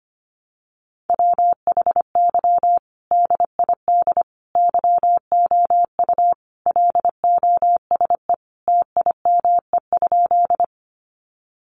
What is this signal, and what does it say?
Morse code, heard as a keyed tone, about 25 wpm, 700 Hz. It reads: WHY DID YOU LOSE TIME?
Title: Sentences from Top 1100 Words 25wpm